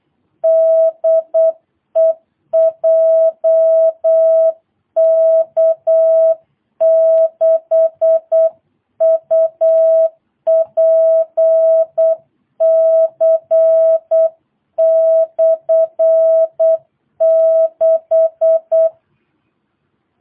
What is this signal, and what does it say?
Morse code, heard as a keyed tone, about 8 wpm, 650 Hz. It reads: DE JK6UPC/6
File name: jh8cbh / 2022cw